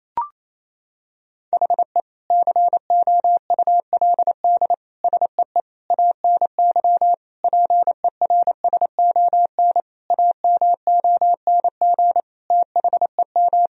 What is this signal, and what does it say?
Morse code, heard as a keyed tone, about 28 wpm, 700 Hz. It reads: HE COULD SEE ANY PERSON AMONG THEM
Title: Sentences from Top 1000 Words - Review Part 3 - 28wpm